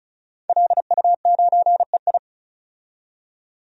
Morse code, heard as a keyed tone, about 34 wpm, 700 Hz. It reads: LU9EI
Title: DXCC-100 Call Signs - 5 Characters - 35wpm